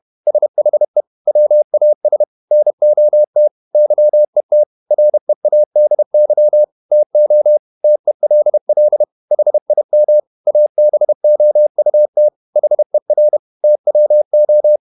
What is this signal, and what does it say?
Morse code, heard as a keyed tone, about 31 wpm, 600 Hz. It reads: SHE WAS NOT YET READY TO TELL HIM ABOUT HER TWO
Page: Never 31wpm | CW med Gnister